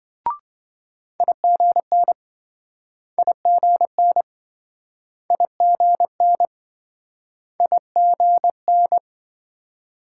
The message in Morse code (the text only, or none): IGN IGN IGN IGN